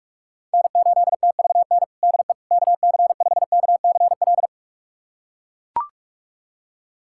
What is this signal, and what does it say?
Morse code, heard as a keyed tone, about 43 wpm, 700 Hz. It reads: N8TVN DE KC5KCL
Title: USA Callsign DE Callsign 45wpm